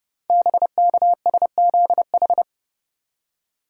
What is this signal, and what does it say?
Morse code, heard as a keyed tone, about 29 wpm, 700 Hz. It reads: BKSZH
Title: Code Groups - 5 Letters - 30wpm